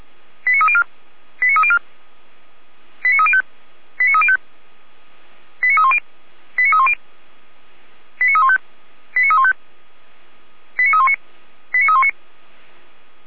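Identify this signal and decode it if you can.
Five-tone selective call (ZVEI), heard as quick runs of five tones, tones of about 70 ms. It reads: 89304 89304 89305 89305 89310 89310 89315 89315 89319 89319
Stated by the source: ZVEI five-tone selective call 89270 received by SDR.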